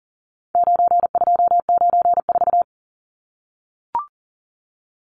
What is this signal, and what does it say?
Morse code, heard as a keyed tone, about 40 wpm, 700 Hz. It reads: 9294
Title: Numbers - 4-Digit 40wpm